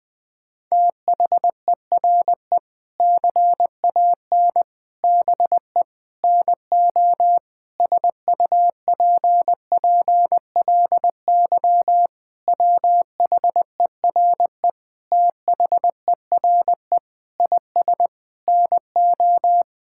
Morse code, heard as a keyed tone, about 20 wpm, 700 Hz. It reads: THERE CAN BE NO SUPPLY WHERE THERE IS NO
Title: Sentences from Top 1000 Words - Review Part 5 - 20wpm